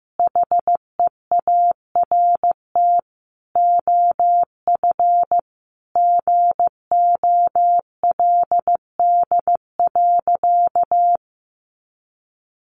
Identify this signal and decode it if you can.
Morse code, heard as a keyed tone, about 15 wpm, 700 Hz. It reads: HEART OF GOLD.
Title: Sentences from Top 1200 Words 15wpm